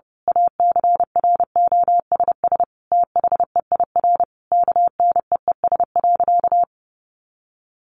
Morse code, heard as a keyed tone, about 30 wpm, 700 Hz. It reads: ACROSS THEIR KNEES.
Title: A Sound Of Thunder 30wpm